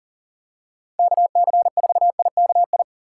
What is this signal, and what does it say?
Morse code, heard as a keyed tone, about 40 wpm, 700 Hz. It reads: KC4IKI
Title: USA Call Signs - 6 Characters - 4x Character Spacing - 40wpm